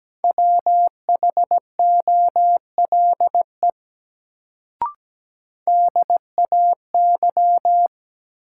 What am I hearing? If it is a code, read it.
Morse code, heard as a keyed tone, about 17 wpm, 700 Hz. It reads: WHOLE DAY
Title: Sets of 2 Words - New Word Drill - 17wpm